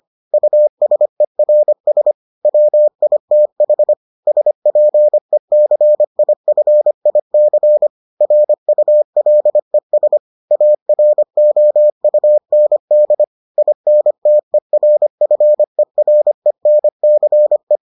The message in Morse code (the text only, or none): USERS WITH SPECIFIC RULES AROUND INTERFERENCE